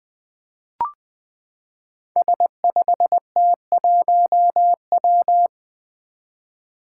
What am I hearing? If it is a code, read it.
Morse code, heard as a keyed tone, about 20 wpm, 700 Hz. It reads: S5T1W
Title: Five Challenging Letter-Number Combinations 20wpm